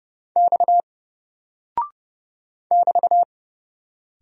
Morse code, heard as a keyed tone, about 29 wpm, 700 Hz. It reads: X <BT>
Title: Single Character (Letter:Number:Punctuation:Prosign) 30wpm